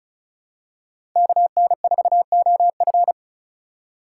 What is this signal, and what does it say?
Morse code, heard as a keyed tone, about 36 wpm, 700 Hz. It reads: KN4OF
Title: USA Call Signs - 5 Characters - 4x Character Spacing - 35wpm